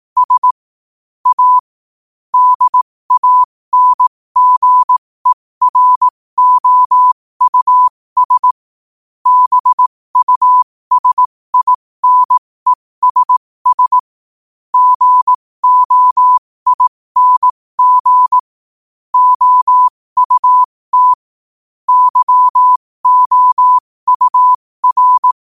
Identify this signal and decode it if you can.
Morse code, heard as a keyed tone, about 18 wpm, spaced as for 15 wpm, 1000 Hz. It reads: S A DANGEROUS BUSINESS GOING OUT YOUR
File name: qotdfiles / 15wpm